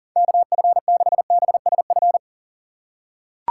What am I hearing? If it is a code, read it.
Morse code, heard as a keyed tone, about 40 wpm, 700 Hz. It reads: KF6BSF
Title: USA Call Signs - 3x Character Spacing - 40wpm